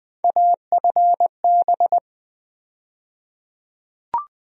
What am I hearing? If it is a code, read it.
Morse code, heard as a keyed tone, about 20 wpm, 700 Hz. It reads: AFB
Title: USA Call Sign Postfixes - 3 Letters 20wpm